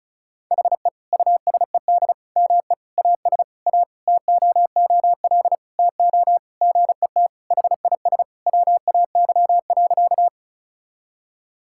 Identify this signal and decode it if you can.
Morse code, heard as a keyed tone, about 36 wpm, 700 Hz. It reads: HE USED ME AS A TOOL TO GET HIS WAY.
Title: Sentences from Top 1000 Words - Review Part 6 - 35wpm